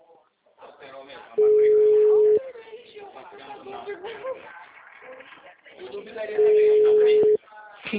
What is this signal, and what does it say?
Ringback tone (Europe, 425 Hz), with speech.